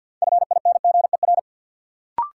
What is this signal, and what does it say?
Morse code, heard as a keyed tone, about 53 wpm, 700 Hz. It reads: FINGER
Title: Sentences from Top 1600 Words - New Word Drill - 50wpm